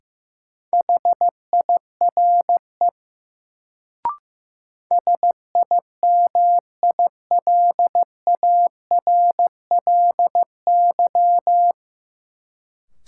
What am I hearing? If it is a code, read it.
Morse code, heard as a keyed tone, about 15 wpm, 700 Hz. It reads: HIRE SIMILARLY
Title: Sentences from Top 1400 Words - New Word Drill - 15wpm